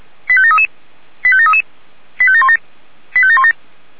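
Five-tone selective call (ZVEI), heard as quick runs of five tones, tones of about 70 ms. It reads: 85733 85733 85717 85717
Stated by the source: ZVEI five-tone selective call received on SDR.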